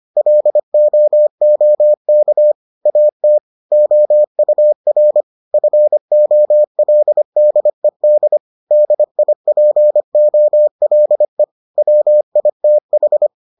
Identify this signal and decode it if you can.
Morse code, heard as a keyed tone, about 25 wpm, 600 Hz. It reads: LOOK AT OUR FOLDED DIPOLE WITH